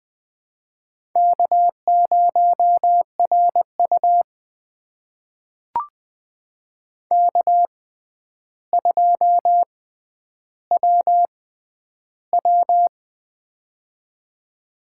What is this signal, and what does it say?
Morse code, heard as a keyed tone, about 20 wpm, spaced as for 10 wpm, 700 Hz. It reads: K0RU K2WW
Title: USA Call Signs - 4 Characters - 6x Character Spacing - 20wpm